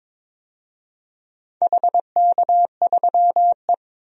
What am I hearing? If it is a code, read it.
Morse code, heard as a keyed tone, about 22 wpm, 700 Hz. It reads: HK3E
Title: DXCC-100 Call Signs - 4 Characters - 22wpm